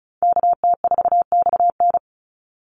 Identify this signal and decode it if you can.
Morse code, heard as a keyed tone, about 34 wpm, 700 Hz. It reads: KT4XN
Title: Call Signs Repeated 3x - 35wpm